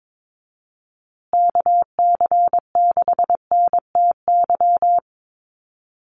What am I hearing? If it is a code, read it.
Morse code, heard as a keyed tone, about 22 wpm, 700 Hz. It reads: KC6NTY
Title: USA Call Signs Repeated 2x 22wpm